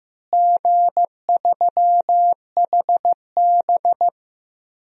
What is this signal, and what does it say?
Morse code, heard as a keyed tone, about 15 wpm, 700 Hz. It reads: G3HB
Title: DXCC-100 Call Signs - 4 Characters - Repeated 2x - 15wpm